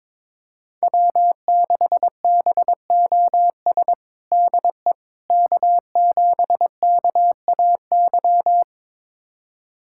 Morse code, heard as a keyed tone, about 22 wpm, 700 Hz. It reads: W6BOS DE K7KAY
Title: USA Callsign DE Callsign - 5 Characters - 22wpm